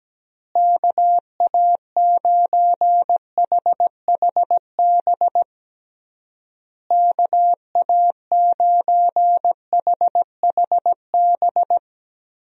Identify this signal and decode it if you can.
Morse code, heard as a keyed tone, about 17 wpm, 700 Hz. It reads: KA9HHB KA9HHB
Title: USA Call Signs - 6 Characters - Repeated 3x - 17wpm